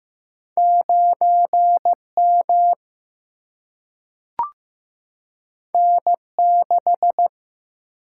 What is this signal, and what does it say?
Morse code, heard as a keyed tone, about 15 wpm, 700 Hz. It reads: 9 M N 6
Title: Single-Letter Single-Number 15wpm